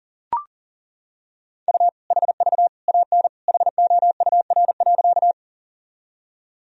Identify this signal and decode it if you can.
Morse code, heard as a keyed tone, about 37 wpm, 700 Hz. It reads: U HV AN HOUR.
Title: Sentences from Top 400 Words and Abbreviations 40wpm